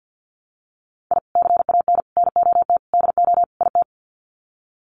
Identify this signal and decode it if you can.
Morse code, heard as a keyed tone, about 48 wpm, 700 Hz. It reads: I CAN NOT DO IT
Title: Sets of 5 Words Encore 50wpm